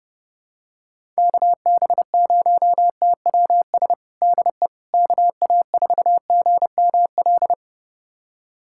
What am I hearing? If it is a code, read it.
Morse code, heard as a keyed tone, about 30 wpm, 700 Hz. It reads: KB0TWS DE KA4GML
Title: USA Callsign DE Callsign - 6 Characters - 30wpm